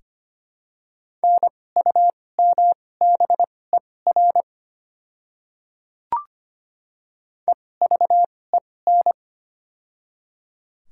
Morse code, heard as a keyed tone, about 25 wpm, spaced as for 18 wpm, 700 Hz. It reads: NUMBER EVEN
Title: 2x Character Spacing - Top 200 Words - 25wpm